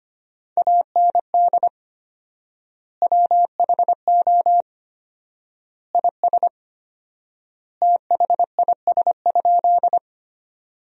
Morse code, heard as a keyed tone, about 25 wpm, 700 Hz. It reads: AND WHO IS THIS?
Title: 4x Word-Spacing - Sentences from Top 100 Words - 25wpm